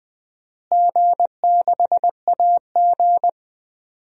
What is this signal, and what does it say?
Morse code, heard as a keyed tone, about 20 wpm, 700 Hz. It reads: G6AG
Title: DXCC-100 Call Signs - 4 Characters - Repeated 3x - 20wpm